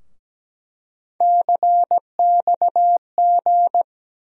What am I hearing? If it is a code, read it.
Morse code, heard as a keyed tone, about 17 wpm, 700 Hz. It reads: CXG
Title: USA Call Sign Postfixes - 3 Letters 17wpm